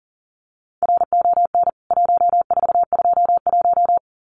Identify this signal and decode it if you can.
Morse code, heard as a keyed tone, about 40 wpm, 700 Hz. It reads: RON 1421
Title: CWT Exchange 40wpm